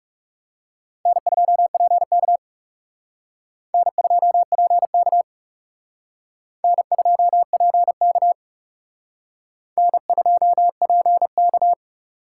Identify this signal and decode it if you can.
Morse code, heard as a keyed tone, about 37 wpm, 700 Hz. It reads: N2PK N2PK N2PK N2PK
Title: Speed Racing - USA 1x2 Call Signs 30wpm